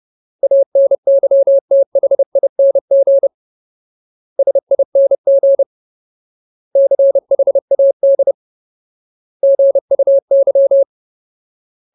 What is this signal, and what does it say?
Morse code, heard as a keyed tone, about 30 wpm, 550 Hz. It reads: ANYTHING SING CHAD GUY
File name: Ultimate Word List / 30WPM Ultimate